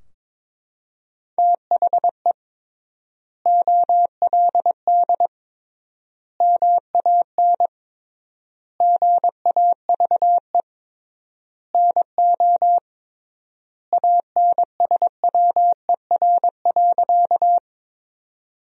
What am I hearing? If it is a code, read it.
Morse code, heard as a keyed tone, about 22 wpm, 700 Hz. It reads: THE OLD MAN GAVE NO ANSWER.
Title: 3x Word-Spacing - Sentences from Top 500 Words - 22wpm